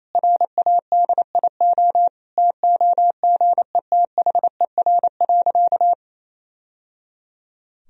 Morse code, heard as a keyed tone, about 27 wpm, 700 Hz. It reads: RADIO TOGETHER.
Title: Sentences from Top 1000 Words - Review Part 3 - 28wpm